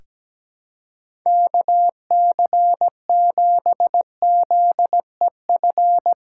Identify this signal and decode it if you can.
Morse code, heard as a keyed tone, about 17 wpm, 700 Hz. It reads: KC7ZEF
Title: USA Call Signs - 6 Characters - 3x Character Spacing - 17wpm